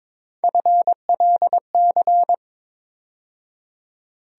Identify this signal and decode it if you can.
Morse code, heard as a keyed tone, about 22 wpm, 700 Hz. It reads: FLC
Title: USA Call Sign Postfixes - 3 Letters 22wpm